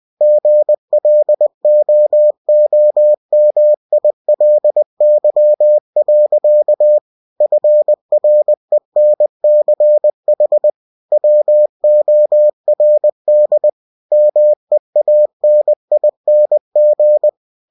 Morse code, heard as a keyed tone, about 20 wpm, 600 Hz. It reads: GLOOMILY. FRENCH WORD MEANING
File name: cw-books / winnie-the-pooh / winnie-20wpm-page-0045